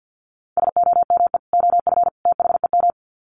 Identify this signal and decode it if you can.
Morse code, heard as a keyed tone, about 50 wpm, 700 Hz. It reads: SOME OF THEM
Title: Sets of 3 Words 50wpm